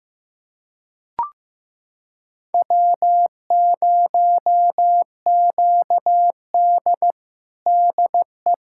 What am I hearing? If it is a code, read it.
Morse code, heard as a keyed tone, about 15 wpm, 700 Hz. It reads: W0QD DE
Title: USA Callsign DE Callsign - 4 Characters - 15wpm